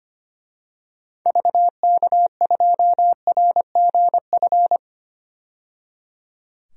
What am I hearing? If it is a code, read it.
Morse code, heard as a keyed tone, about 25 wpm, 690 Hz. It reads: VK2RGF
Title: DXCC-100 Call Signs - 6 Characters - 25wpm